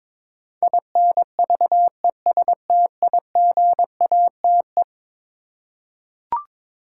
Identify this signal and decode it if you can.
Morse code, heard as a keyed tone, about 22 wpm, 700 Hz. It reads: INVESTIGATE
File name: practice / 22wpm / Eleven Letter Words - 22wpm